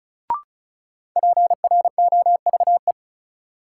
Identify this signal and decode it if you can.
Morse code, heard as a keyed tone, about 36 wpm, 700 Hz. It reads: PROVE
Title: Sentences from Top 1600 Words - New Word Drill Alt - 35wpm